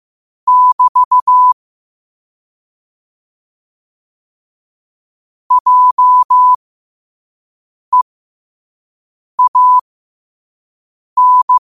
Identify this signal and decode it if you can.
Morse code, heard as a keyed tone, about 15 wpm, spaced as for 5 wpm, 1000 Hz. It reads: <BT> JEAN
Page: New quotes every day in morse code at 5 Words per minute.